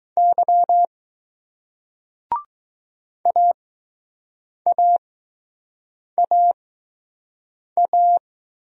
Morse code, heard as a keyed tone, about 20 wpm, spaced as for 6 wpm, 700 Hz. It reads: Y AAAA